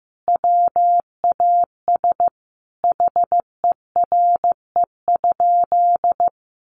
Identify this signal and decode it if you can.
Morse code, heard as a keyed tone, about 15 wpm, 700 Hz. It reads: WAS HERE?
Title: Sentences from Top 1500 Words 15wpm